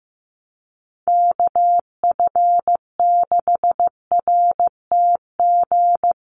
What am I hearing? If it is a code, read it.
Morse code, heard as a keyed tone, about 15 wpm, 700 Hz. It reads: KF6RTG
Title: USA Call Signs Repeated 3x 15wpm